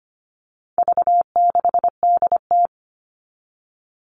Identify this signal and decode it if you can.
Morse code, heard as a keyed tone, about 25 wpm, 700 Hz. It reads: V6DT
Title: All Cut Numbers - 4-Digit 25wpm